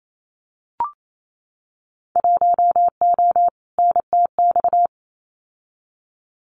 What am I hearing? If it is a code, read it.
Morse code, heard as a keyed tone, about 28 wpm, 700 Hz. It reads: 1O NTX
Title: Winter Field Day Exchanges 28wpm